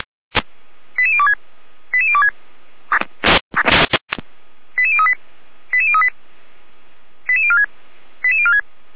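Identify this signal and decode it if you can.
Five-tone selective call (ZVEI), heard as quick runs of five tones, tones of about 70 ms. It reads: 88026 88026 88038 88038 88046 88046
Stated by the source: Received on an SDR, from the Feuerwehr Orlamünde audio archive.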